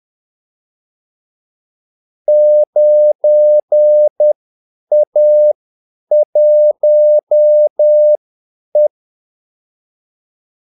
Morse code, heard as a keyed tone, about 10 wpm, spaced as for 8 wpm, 600 Hz. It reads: 9A1E
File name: Callsigns-10-8wpm